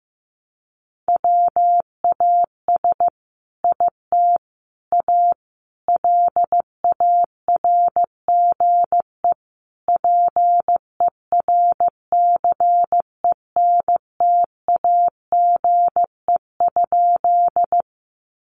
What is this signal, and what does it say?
Morse code, heard as a keyed tone, about 15 wpm, 700 Hz. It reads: WAS IT A LARGE PERCENTAGE?
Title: Sentences from Top 1300 Words 15wpm